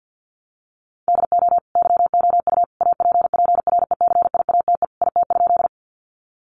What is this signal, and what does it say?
Morse code, heard as a keyed tone, about 53 wpm, 700 Hz. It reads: DO YOU APPRECIATE IT?
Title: Sentences from Top 1300 Words 50wpm